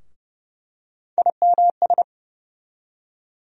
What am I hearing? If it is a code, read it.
Morse code, heard as a keyed tone, about 30 wpm, 700 Hz. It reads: IMS
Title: USA Call Sign Postfixes - 3 Letters 30wpm